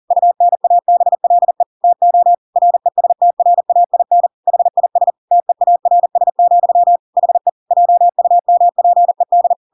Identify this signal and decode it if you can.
Morse code, heard as a keyed tone, about 40 wpm, 700 Hz. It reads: UNABLE TO RESTRAIN HIS TEARS, HE JUMPED
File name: Pinocchio / Pinocchio ch8 40 40 ik1hge